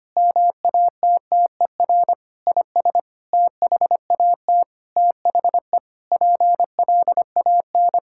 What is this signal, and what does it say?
Morse code, heard as a keyed tone, about 25 wpm, 700 Hz. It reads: MATTER IS THAT THE PLAN
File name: practice / 25wpm / Sentences with 9-Word Phrases - 25wpm